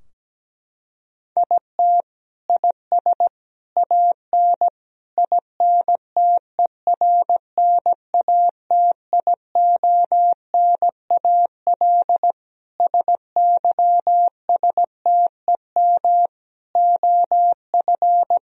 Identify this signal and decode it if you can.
Morse code, heard as a keyed tone, about 17 wpm, 700 Hz. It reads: IT IS AN INTERNATIONAL SYSTEM OF
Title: Sentences from Top 1000 Words - Review Part 4 - 17wpm